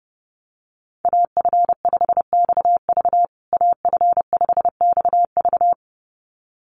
Morse code, heard as a keyed tone, about 30 wpm, 700 Hz. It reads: AF5XV AF5XV
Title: USA Call Signs Repeated 2x 30wpm